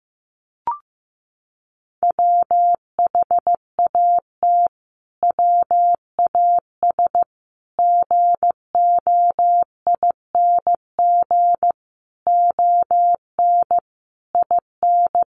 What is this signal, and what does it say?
Morse code, heard as a keyed tone, about 15 wpm, 700 Hz. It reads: WHAT WAS GOING ON IN
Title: Sets of 5 Words Encore 15wpm